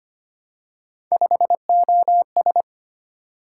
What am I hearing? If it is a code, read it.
Morse code, heard as a keyed tone, about 25 wpm, 700 Hz. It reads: 5OS
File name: Three Letter-Number Combinations - 25wpm